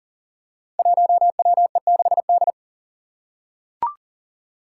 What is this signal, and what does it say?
Morse code, heard as a keyed tone, about 40 wpm, 700 Hz. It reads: 1WE6D
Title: Five Challenging Letter-Number Combinations 40wpm